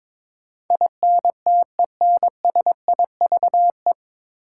Morse code, heard as a keyed tone, about 22 wpm, 700 Hz. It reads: INTENSIVE
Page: Morse Code Ninja practice 22wpm